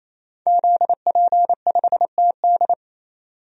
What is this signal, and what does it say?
Morse code, heard as a keyed tone, about 27 wpm, 700 Hz. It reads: ZP5TD
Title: DXCC-100 Call Signs - 5 Characters - 28wpm